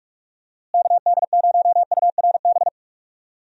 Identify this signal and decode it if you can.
Morse code, heard as a keyed tone, about 44 wpm, 700 Hz. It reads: KD0URB
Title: USA Call Signs - 6 Characters - Repeated 3x - 45wpm